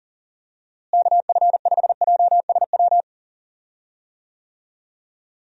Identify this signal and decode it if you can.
Morse code, heard as a keyed tone, about 37 wpm, 700 Hz. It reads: KF5JSW
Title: USA Call Signs - 6 Characters - Repeated 3x - 40wpm